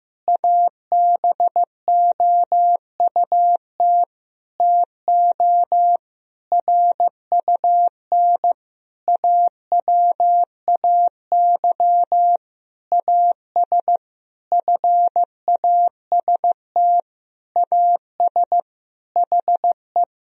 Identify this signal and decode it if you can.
Morse code, heard as a keyed tone, about 16 wpm, 700 Hz. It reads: ABOUT TO RUN AWAY AS FAST AS HE
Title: Sentences from Top 1000 Words - Review Part 2 - 15wpm